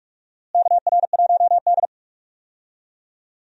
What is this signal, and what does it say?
Morse code, heard as a keyed tone, about 46 wpm, 700 Hz. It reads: KR1D
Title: USA Call Signs - 4 Characters - Repeated 3x - 45wpm